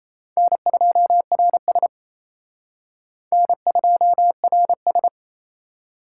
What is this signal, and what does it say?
Morse code, heard as a keyed tone, about 35 wpm, 700 Hz. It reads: N2RS N2RS
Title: Speed Racing - USA 1x2 Call Signs 28wpm